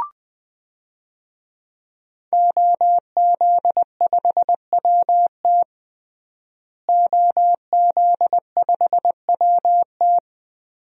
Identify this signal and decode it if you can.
Morse code, heard as a keyed tone, about 20 wpm, 700 Hz. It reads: OZ5WT OZ5WT
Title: DXCC-100 Call Signs - 5 Characters - Repeated 3x - 20wpm